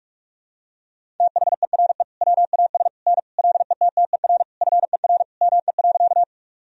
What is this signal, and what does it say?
Morse code, heard as a keyed tone, about 46 wpm, 700 Hz. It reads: THERE WAS N LETTER FER ME.
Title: Sentences from Top 300 Words and Abbreviations 45wpm